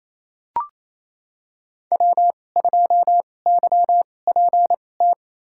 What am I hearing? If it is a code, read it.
Morse code, heard as a keyed tone, about 28 wpm, spaced as for 20 wpm, 700 Hz. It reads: W2YPT
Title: USA Call Signs - 2x Character Spacing - 28wpm